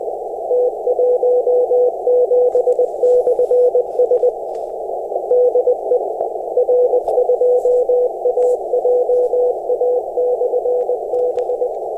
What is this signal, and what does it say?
Morse code, heard as a keyed tone, about 20 wpm, 500 Hz, with noise. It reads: T17/S DE R2AJA/B
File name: attachments / 16.23.03 r2ajab